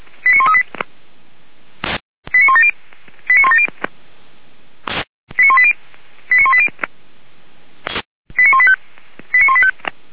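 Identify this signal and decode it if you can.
Five-tone selective call (ZVEI), heard as quick runs of five tones, tones of about 70 ms. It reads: 89138 89170 89170 89180 89180 89185 89185
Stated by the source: ZVEI five-tone call received on an SDR by feuerwehr-orlamuende (fire brigade Orlamünde).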